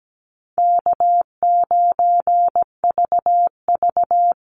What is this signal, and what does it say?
Morse code, heard as a keyed tone, about 17 wpm, 700 Hz. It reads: K9VV
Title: Easy Call Signs - 17wpm